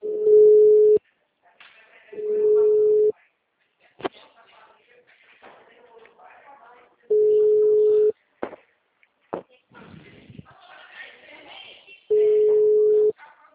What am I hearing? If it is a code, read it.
Ringback tone (Europe, 425 Hz), with speech.
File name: out- -2006- - - .7716